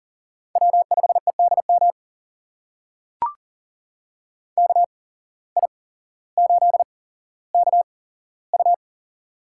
Morse code, heard as a keyed tone, about 40 wpm, spaced as for 17 wpm, 700 Hz. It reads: W5EDM KI8KU
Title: USA Call Signs - 5 Characters - 8x Character Spacing - 40wpm